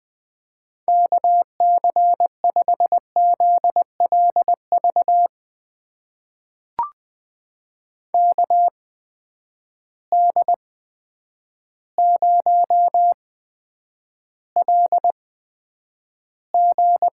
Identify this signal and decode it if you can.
Morse code, heard as a keyed tone, about 20 wpm, spaced as for 9 wpm, 700 Hz. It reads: KC5ZLV KD0LG
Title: USA Call Signs - 6 Characters - 8x Character Spacing - 20wpm